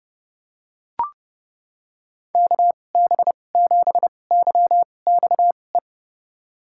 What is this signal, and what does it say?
Morse code, heard as a keyed tone, about 30 wpm, spaced as for 22 wpm, 700 Hz. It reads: KB7YXE
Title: USA Call Signs - 6 Characters - 2x Character Spacing - 30wpm